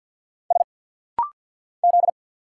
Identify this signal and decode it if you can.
Morse code, heard as a keyed tone, about 50 wpm, 700 Hz. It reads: S Z
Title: Single Letters - Rapid-Fire 50wpm